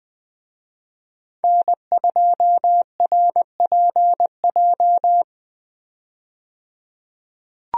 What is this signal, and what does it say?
Morse code, heard as a keyed tone, about 20 wpm, 700 Hz. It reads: N2RPJ
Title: USA Call Signs - 5 Characters - 20wpm